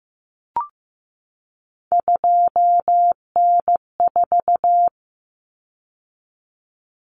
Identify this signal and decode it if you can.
Morse code, heard as a keyed tone, about 15 wpm, 700 Hz. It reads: 2N4
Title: Common Cut Numbers - 3-Digit 15wpm